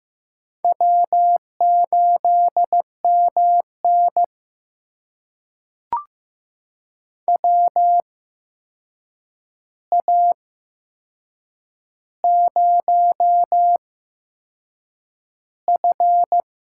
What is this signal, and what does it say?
Morse code, heard as a keyed tone, about 15 wpm, spaced as for 6 wpm, 700 Hz. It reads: W8MN WA0F
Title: USA Call Signs - 4 Characters - 8x Character Spacing - 15wpm